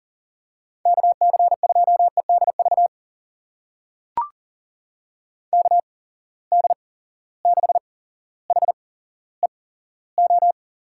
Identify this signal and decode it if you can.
Morse code, heard as a keyed tone, about 41 wpm, spaced as for 17 wpm, 700 Hz. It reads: KC2EDV KD6HEO
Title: USA Call Signs - 6 Characters - 8x Character Spacing - 40wpm